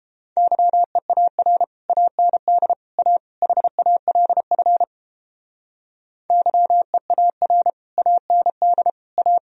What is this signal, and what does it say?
Morse code, heard as a keyed tone, about 31 wpm, 700 Hz. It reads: YEAR AND A HALF YEAR AND A